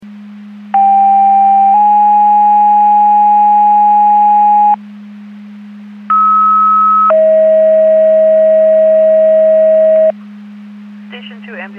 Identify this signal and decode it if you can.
Two-tone page, heard as a one-second tone, then a three-second tone, about 790 then 830 Hz, about 1290 then 640 Hz, with speech. It is heard from 0.7 to 10.1 s.